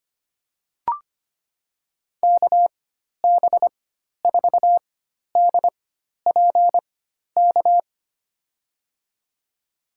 Morse code, heard as a keyed tone, about 25 wpm, spaced as for 12 wpm, 700 Hz. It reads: KB4DPK
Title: USA Call Signs - 4x Character Spacing - 25wpm